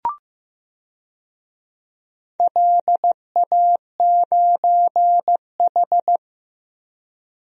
Morse code, heard as a keyed tone, about 15 wpm, 700 Hz. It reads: LA9H
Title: DXCC-100 Call Signs - 4 Characters - Repeated 2x - 15wpm